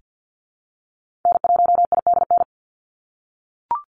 Morse code, heard as a keyed tone, about 50 wpm, 700 Hz. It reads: N1IDN
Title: NA SOTA Activator Callsigns 50wpm